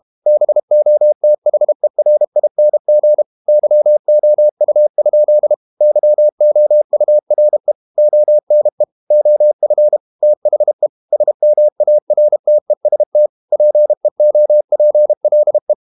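Morse code, heard as a keyed tone, about 32 wpm, 600 Hz. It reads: BOTHERING YOU? YOURE ONE OF THE SMARTEST PEOPLE
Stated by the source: Morse code practice at 32 wpm from CW med Gnister, reading Ken Follett's novel Never in English.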